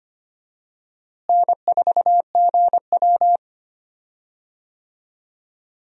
Morse code, heard as a keyed tone, about 25 wpm, 700 Hz. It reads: N4GW
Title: USA Call Signs - 4 Characters - Repeated 2x - 25wpm